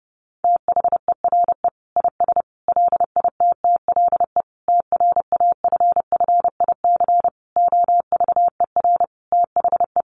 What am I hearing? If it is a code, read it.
Morse code, heard as a keyed tone, about 30 wpm, 700 Hz. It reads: THERE IS LITTLE TRAFFIC OVER THE
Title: Sentences from Top 1300 Words 30wpm